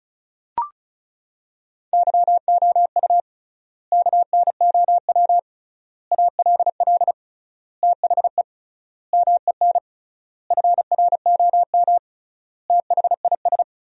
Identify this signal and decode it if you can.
Morse code, heard as a keyed tone, about 36 wpm, 700 Hz. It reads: YOU KNOW ALL THE MEN FROM THIS
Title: 3x Word-Spacing - Sentences from Top 300 Words - 35wpm